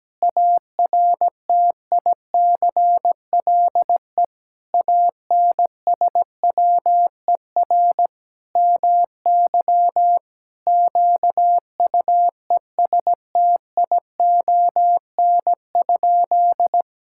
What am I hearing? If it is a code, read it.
Morse code, heard as a keyed tone, about 17 wpm, 700 Hz. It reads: ARTICLE ANSWER MY QUESTION?
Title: Sentences from Top 1000 Words - Review Part 5 - 17wpm